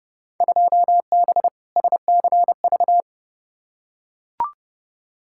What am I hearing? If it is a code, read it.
Morse code, heard as a keyed tone, about 30 wpm, 700 Hz. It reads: 2B SCV
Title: Field Day Exchanges Encore - 30wpm